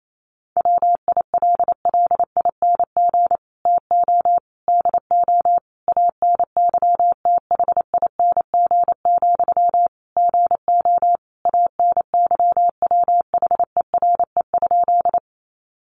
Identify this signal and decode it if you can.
Morse code, heard as a keyed tone, about 28 wpm, 700 Hz. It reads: WILLING TO DO ANYTHING, GO ANYWHERE?
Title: Sentences from Top 1400 Words 28wpm